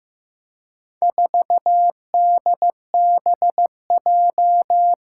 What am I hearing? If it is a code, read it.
Morse code, heard as a keyed tone, about 15 wpm, 700 Hz. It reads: 4DBJ
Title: Four Challenging Letter-Number Combinations 15wpm